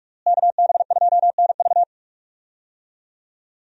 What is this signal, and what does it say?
Morse code, heard as a keyed tone, about 46 wpm, 700 Hz. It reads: KB2NV
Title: USA Call Signs - 5 Characters - Repeated 2x - 45wpm